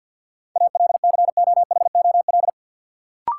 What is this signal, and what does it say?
Morse code, heard as a keyed tone, about 50 wpm, 700 Hz. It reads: ALCOHOL
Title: Sentences from Top 1400 Words - New Word Drill - 50wpm